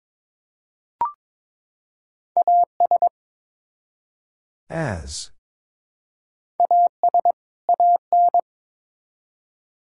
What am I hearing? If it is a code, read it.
Morse code, heard as a keyed tone, about 22 wpm, 700 Hz. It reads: AS AS AN
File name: Build A Phrase - 2-Word Phrases - 22wpm